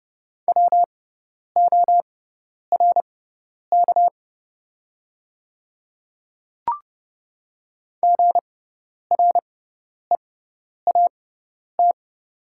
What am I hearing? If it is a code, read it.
Morse code, heard as a keyed tone, about 30 wpm, spaced as for 10 wpm, 700 Hz. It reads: WORK GREAT
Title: 6x Character Spacing - Top 200 Words - 30wpm